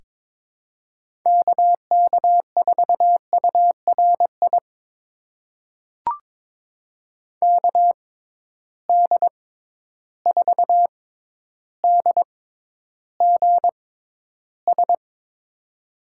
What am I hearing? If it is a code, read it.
Morse code, heard as a keyed tone, about 22 wpm, spaced as for 11 wpm, 700 Hz. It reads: KK4URI KD4DGS
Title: USA Call Signs - 6 Characters - 6x Character Spacing - 22wpm